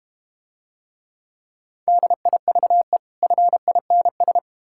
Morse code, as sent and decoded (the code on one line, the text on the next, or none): -.. .. ...- . / ..-. .. -. ...
DIVE FINS